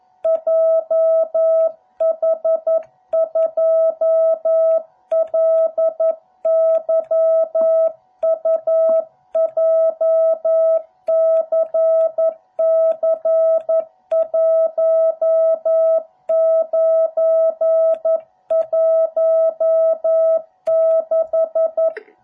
Morse code, heard as a keyed tone, about 11 wpm, 660 Hz. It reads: JH2LYU JCC1916